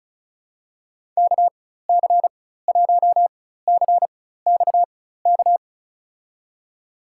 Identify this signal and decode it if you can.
Morse code, heard as a keyed tone, about 36 wpm, spaced as for 16 wpm, 700 Hz. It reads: KC1CXK
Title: USA Call Signs - 4x Character Spacing - 35wpm